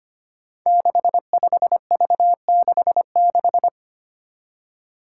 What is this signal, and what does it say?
Morse code, heard as a keyed tone, about 25 wpm, 700 Hz. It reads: 65V66
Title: Five Challenging Letter-Number Combinations 25wpm